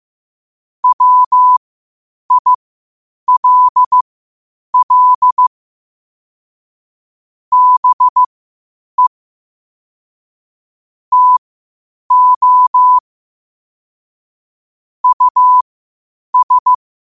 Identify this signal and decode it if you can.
Morse code, heard as a keyed tone, about 15 wpm, spaced as for 8 wpm, 1000 Hz. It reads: WILL BE TO US
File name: qotdfiles / 8wpm